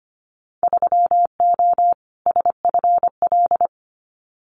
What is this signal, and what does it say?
Morse code, heard as a keyed tone, about 25 wpm, 700 Hz. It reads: 3O SFL
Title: Winter Field Day Exchanges 25wpm